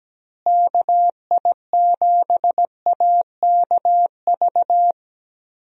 Morse code, heard as a keyed tone, about 17 wpm, 700 Hz. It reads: KI7AKV
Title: USA Call Signs - 6 Characters - Repeated 2x - 17wpm